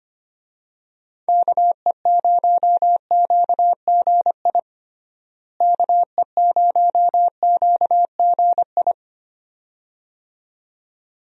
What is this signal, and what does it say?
Morse code, heard as a keyed tone, about 25 wpm, 700 Hz. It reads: KE0QGI KE0QGI
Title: USA Call Signs - 6 Characters - Repeated 2x - 25wpm